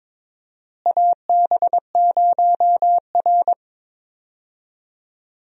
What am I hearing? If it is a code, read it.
Morse code, heard as a keyed tone, about 22 wpm, 700 Hz. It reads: AB0R
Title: USA Call Signs - 4 Characters - Repeated 2x - 22wpm